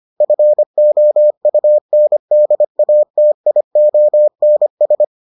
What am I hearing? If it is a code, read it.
Morse code, heard as a keyed tone, about 25 wpm, 600 Hz. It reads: FOUNDATIONS